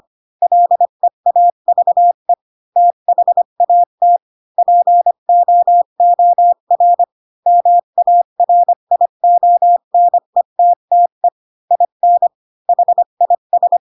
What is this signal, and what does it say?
Morse code, heard as a keyed tone, about 25 wpm, 700 Hz. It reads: LEAVE THAT POOR MARIONETTE IN HIS
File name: Pinocchio / Pinocchio ch3 20 25 ik1hge